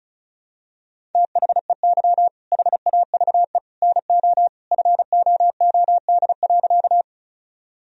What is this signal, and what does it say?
Morse code, heard as a keyed tone, about 34 wpm, 700 Hz. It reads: THEY HAVE NO FOOD.
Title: Sentences from Top 1000 Words - Review Part 1 - 35wpm